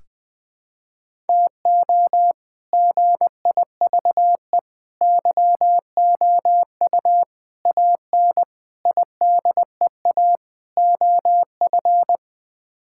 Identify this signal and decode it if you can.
Morse code, heard as a keyed tone, about 20 wpm, 700 Hz. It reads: TO GIVE YOU AN IDEA OF
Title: Sets of 6 Words - 20wpm